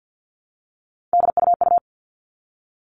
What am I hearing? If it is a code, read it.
Morse code, heard as a keyed tone, about 50 wpm, 700 Hz. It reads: DUU
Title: All Cut Numbers - 3-Digit 50wpm